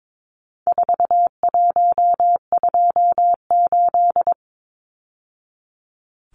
Morse code, heard as a keyed tone, about 22 wpm, 700 Hz. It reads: 4128
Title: Numbers - 4-Digit 22wpm